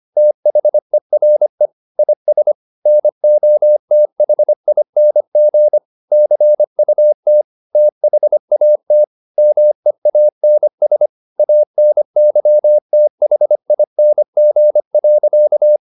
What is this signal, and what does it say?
Morse code, heard as a keyed tone, about 25 wpm, 600 Hz. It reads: THERE IS NOTHING CUT THAT MEANS ANYTHING.